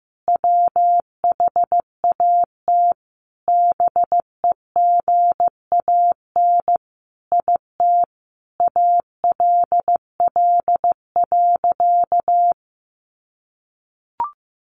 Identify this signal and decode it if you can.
Morse code, heard as a keyed tone, about 15 wpm, 700 Hz. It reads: WHAT BEGAN IT ALL.
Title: Sentences from Top 300 Words 15wpm